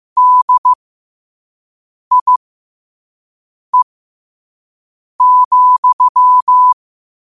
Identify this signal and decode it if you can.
Morse code, heard as a keyed tone, about 15 wpm, spaced as for 5 wpm, 1000 Hz. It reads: DIE,
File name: qotdfiles / 5wpm